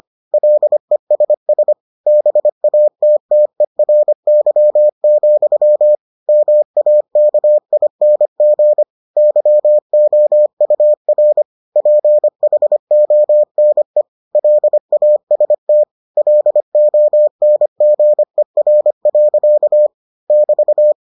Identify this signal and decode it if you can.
Morse code, heard as a keyed tone, about 25 wpm, 600 Hz. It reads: LESS BATTERY, MAKING YOUR PHONE LAST LONGER. <BT>